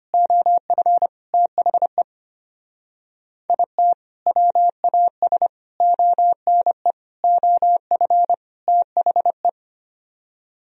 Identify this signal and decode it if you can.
Morse code, heard as a keyed tone, about 32 wpm, 700 Hz. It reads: OF THE IT WAS ONE OF THE